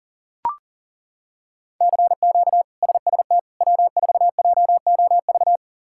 Morse code, heard as a keyed tone, about 40 wpm, 700 Hz. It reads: CQ SST W4JOV
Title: K1USN SST QSO Practice - 40wpm